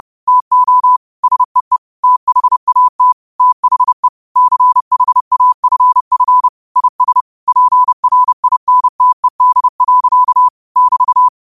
Morse code, heard as a keyed tone, about 30 wpm, 1000 Hz. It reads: TO SEE THAT THE CHAFF IS PRINTED. <BT>